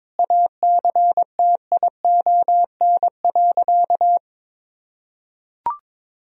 Morse code, heard as a keyed tone, about 22 wpm, 700 Hz. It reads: ACTION.
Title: Sentences from Top 1000 Words - Review Part 4 - 22wpm